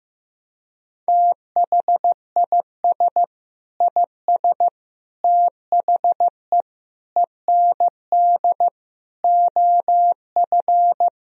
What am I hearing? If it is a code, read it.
Morse code, heard as a keyed tone, about 15 wpm, 700 Hz. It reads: THIS IS THE END OF
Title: Sentences from Top 1000 Words - Review Part 1 - 15wpm